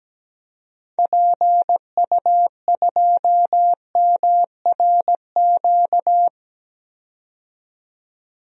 Morse code, heard as a keyed tone, about 17 wpm, 700 Hz. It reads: PU2MRQ
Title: DXCC-100 Call Signs - 6 Characters - 17wpm